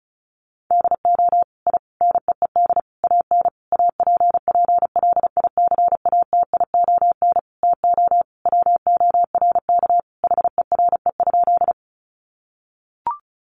Morse code, heard as a keyed tone, about 34 wpm, 700 Hz. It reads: DO I NEED AN APPLICATION TO WORK HERE?
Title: Sentences from Top 1200 Words 35wpm